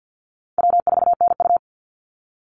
Morse code, heard as a keyed tone, about 50 wpm, 700 Hz. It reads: W4NU
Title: Easy Call Signs Repeated 2x - 50wpm